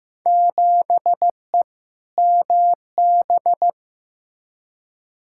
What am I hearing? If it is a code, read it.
Morse code, heard as a keyed tone, about 15 wpm, 700 Hz. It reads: 7E MB
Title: Field Day Exchanges Encore - 15wpm